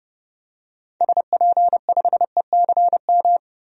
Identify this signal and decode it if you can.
Morse code, heard as a keyed tone, about 30 wpm, 700 Hz. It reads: SP5ECM
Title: DXCC-100 Call Signs - 6 Characters - 30wpm